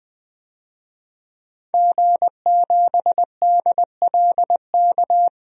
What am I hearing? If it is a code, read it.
Morse code, heard as a keyed tone, about 21 wpm, 700 Hz. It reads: G7DLK
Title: DXCC-100 Call Signs - 5 Characters - Repeated 2x - 20wpm